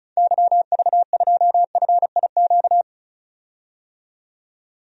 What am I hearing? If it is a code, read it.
Morse code, heard as a keyed tone, about 36 wpm, 700 Hz. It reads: YV2FIQ
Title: DXCC-100 Call Signs - 6 Characters - 35wpm